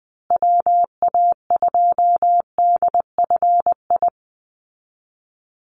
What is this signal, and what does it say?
Morse code, heard as a keyed tone, about 20 wpm, 700 Hz. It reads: WA2DFI
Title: Call Signs Repeated 2x - 20wpm